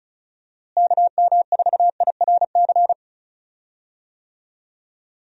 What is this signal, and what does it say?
Morse code, heard as a keyed tone, about 34 wpm, 700 Hz. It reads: KM4IRC
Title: USA Call Signs - 6 Characters - Repeated 3x - 35wpm